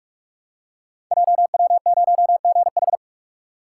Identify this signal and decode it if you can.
Morse code, heard as a keyed tone, about 44 wpm, 700 Hz. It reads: JW0GH
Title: Code Groups - 5 Letter:Numbers - 45wpm